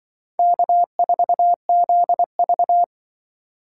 Morse code, heard as a keyed tone, about 25 wpm, 700 Hz. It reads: K4ZV
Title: Speed Racing - USA 1x2 Call Signs 20wpm